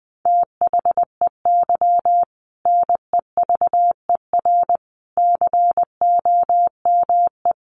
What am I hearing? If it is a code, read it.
Morse code, heard as a keyed tone, about 20 wpm, 700 Hz. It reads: THEY NEVER COME
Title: Sentences from Top 300 Words 20wpm